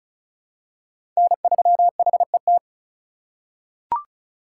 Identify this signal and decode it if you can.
Morse code, heard as a keyed tone, about 36 wpm, 700 Hz. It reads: N3HET
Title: USA Call Signs - 5 Characters - 2x Character Spacing - 35wpm